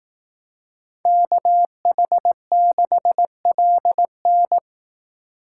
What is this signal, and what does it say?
Morse code, heard as a keyed tone, about 18 wpm, 700 Hz. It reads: KH6LN
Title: Speed Racing - USA 2x2 Call Signs 15wpm